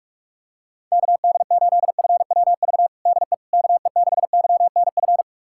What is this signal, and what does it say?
Morse code, heard as a keyed tone, about 44 wpm, 700 Hz. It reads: KD8FWV DE KE6YNF
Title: USA Callsign DE Callsign 45wpm